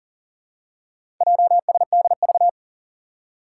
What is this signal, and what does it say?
Morse code, heard as a keyed tone, about 40 wpm, 700 Hz. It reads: JSDV
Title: Four Challenging Letter-Number Combinations 40wpm